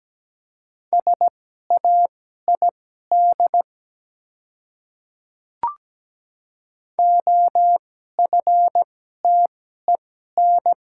Morse code, heard as a keyed tone, about 17 wpm, spaced as for 12 wpm, 700 Hz. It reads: SAID OFTEN
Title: 2x Character Spacing - Top 300 Words - 17wpm